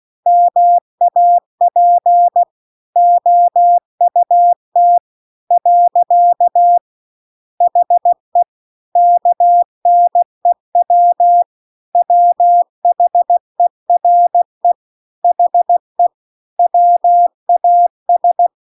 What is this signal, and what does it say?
Morse code, heard as a keyed tone, about 16 wpm, 700 Hz. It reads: MAP OUT . HE KNEW WHERE HE WAS